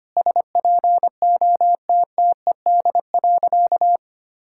Morse code, heard as a keyed tone, about 25 wpm, 700 Hz. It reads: SPOTTED.
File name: practice / 25wpm / All-Age Appropriate Jokes - 25wpm